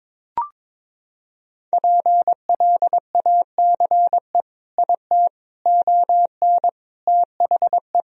Morse code, heard as a keyed tone, about 22 wpm, 700 Hz. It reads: PLACE IT ON THE
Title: Sentences from Top 1000 Words - Review Part 2 - 22wpm